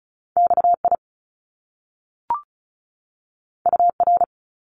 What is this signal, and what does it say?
Morse code, heard as a keyed tone, about 36 wpm, 700 Hz. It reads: XI UR
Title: Top Two Letter Combinations in Words 35wpm